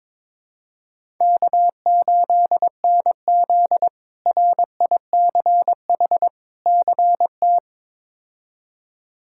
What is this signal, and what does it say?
Morse code, heard as a keyed tone, about 22 wpm, 700 Hz. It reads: K8NZ RICH CT
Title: K1USN SST QSO Practice - 22wpm